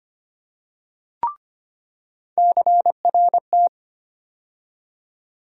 Morse code, heard as a keyed tone, about 25 wpm, 700 Hz. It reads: CRT
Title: USA Call Sign Postfixes - 3 Letters 25wpm